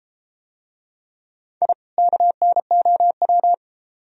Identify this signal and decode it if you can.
Morse code, heard as a keyed tone, about 32 wpm, 700 Hz. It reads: I KNOW